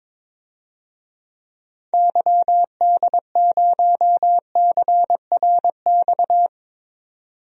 Morse code, heard as a keyed tone, about 22 wpm, 700 Hz. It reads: YD0CRX
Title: DXCC-100 Call Signs - 6 Characters - 22wpm